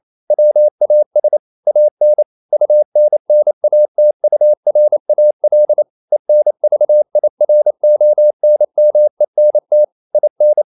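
Morse code, heard as a keyed tone, about 28 wpm, 600 Hz. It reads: WAS AN UNNATURAL ENVIRONMENT IN
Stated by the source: Morse code practice at 28 wpm, text of Ken Follett's novel Never in English, from CW med Gnister.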